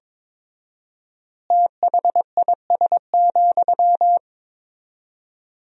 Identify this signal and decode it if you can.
Morse code, heard as a keyed tone, about 22 wpm, 700 Hz. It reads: THIS,
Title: 4x Word-Spacing - Sentences from Top 400 Words - 22wpm